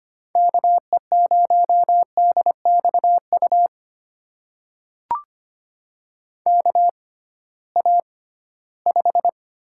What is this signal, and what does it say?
Morse code, heard as a keyed tone, about 25 wpm, spaced as for 16 wpm, 700 Hz. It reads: KE0DXU KA5
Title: USA Call Signs - 6 Characters - 6x Character Spacing - 25wpm